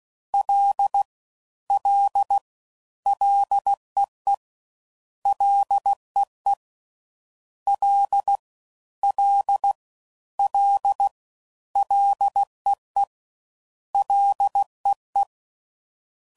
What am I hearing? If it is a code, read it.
Morse code, heard as a keyed tone, about 16 wpm, 800 Hz. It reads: L L LEE LEE L L L LEE LEE
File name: Prefixes / FLs & THs / Single Fs Ls